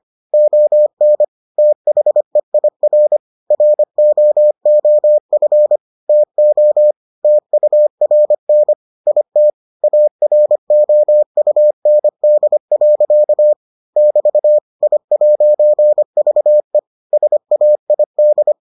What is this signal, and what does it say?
Morse code, heard as a keyed tone, about 25 wpm, 600 Hz. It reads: ON THEIR ROOF TO TURN IT AROUND. <BT> I'VE SAID